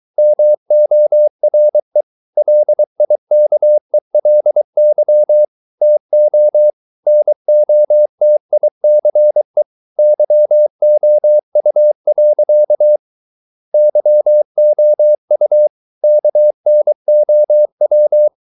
Morse code, heard as a keyed tone, about 23 wpm, 600 Hz. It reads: MORE LIKELY TO NOTICE YOU. YOU KNOW